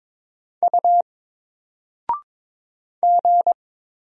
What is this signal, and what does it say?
Morse code, heard as a keyed tone, about 22 wpm, 700 Hz. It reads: U G
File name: Single-Letter-Number-22wpm